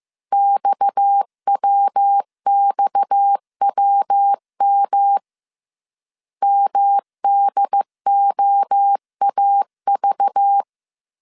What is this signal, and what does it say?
Morse code, heard as a keyed tone, about 15 wpm, 800 Hz. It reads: XWXWM MDOAV